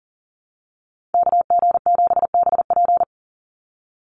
Morse code, heard as a keyed tone, about 40 wpm, 700 Hz. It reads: KG7BP
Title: USA Call Signs 40wpm